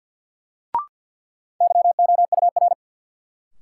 Morse code, heard as a keyed tone, about 48 wpm, 700 Hz. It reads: YOUR
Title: Sets of 2 Words - New Word Drill - 50wpm